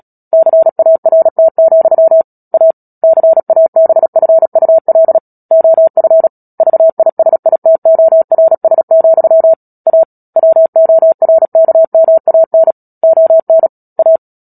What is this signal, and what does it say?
Morse code, heard as a keyed tone, about 37 wpm, 650 Hz. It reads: CART, A CABFUL OF VISITORS, A WORKMAN ON A
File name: Book Two / 10-Chapter 10 - Epilogue - 36 WPM